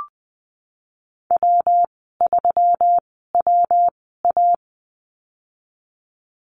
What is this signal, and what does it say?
Morse code, heard as a keyed tone, about 19 wpm, spaced as for 14 wpm, 700 Hz. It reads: W3WA
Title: 167 Lesson 27g - Number 7 - Call Signs 20wpm